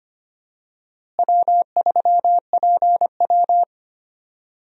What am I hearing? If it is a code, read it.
Morse code, heard as a keyed tone, about 25 wpm, 700 Hz. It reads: W3PW
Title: USA Call Signs - 4 Characters - Repeated 3x - 25wpm